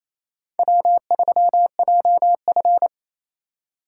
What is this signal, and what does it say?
Morse code, heard as a keyed tone, about 36 wpm, 700 Hz. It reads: W3JF